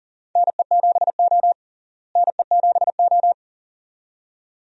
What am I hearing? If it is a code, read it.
Morse code, heard as a keyed tone, about 40 wpm, 700 Hz. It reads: NE7O NE7O
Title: USA Call Signs - 4 Characters - Repeated 2x - 40wpm